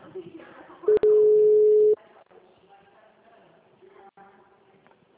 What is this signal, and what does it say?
Ringback tone (Europe, 425 Hz), with speech.
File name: out- -2006- - - .42033